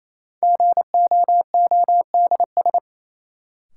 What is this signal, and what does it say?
Morse code, heard as a keyed tone, about 28 wpm, 700 Hz. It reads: GOODS
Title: Sentences from Top 1000 Words - Review Part 6 - New Word Drill - 28wpm